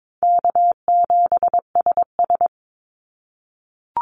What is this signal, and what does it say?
Morse code, heard as a keyed tone, about 22 wpm, 700 Hz. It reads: K7SS
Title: Easy Call Signs - 22wpm